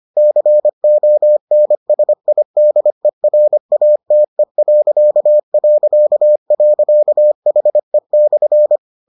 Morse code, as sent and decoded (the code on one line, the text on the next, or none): -.-. --- -. ... .. -.. .-. .- - . .-.-.- .-.-.- .-.-.- .... -..-.
CONSIDERATE...HE/